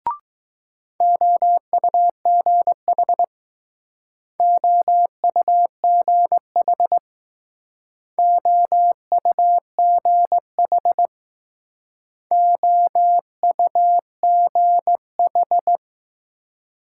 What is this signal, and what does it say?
Morse code, heard as a keyed tone, about 20 wpm, 700 Hz. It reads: OUGH OUGH OUGH OUGH